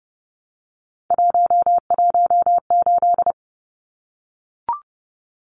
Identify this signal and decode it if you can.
Morse code, heard as a keyed tone, about 30 wpm, 700 Hz. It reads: 118
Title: Numbers - 3-Digit 30wpm